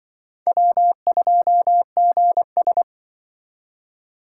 Morse code, heard as a keyed tone, about 25 wpm, 700 Hz. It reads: W2GS